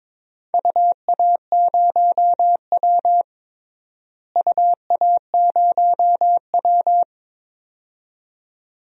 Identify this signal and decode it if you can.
Morse code, heard as a keyed tone, about 22 wpm, 700 Hz. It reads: UA0W UA0W
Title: DXCC-100 Call Signs - 4 Characters - Repeated 2x - 22wpm